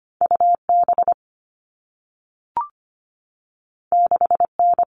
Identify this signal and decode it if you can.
Morse code, heard as a keyed tone, about 25 wpm, 700 Hz. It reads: UB 6N
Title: All Cut Numbers - 2-Digit 25wpm